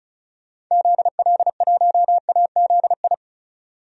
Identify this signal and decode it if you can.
Morse code, heard as a keyed tone, about 34 wpm, 700 Hz. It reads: ZL1AZI
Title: DXCC-100 Call Signs - 6 Characters - 35wpm